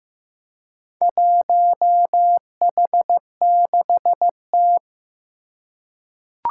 Morse code, heard as a keyed tone, about 15 wpm, 700 Hz. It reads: 1H6T
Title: Four Challenging Letter-Number Combinations 15wpm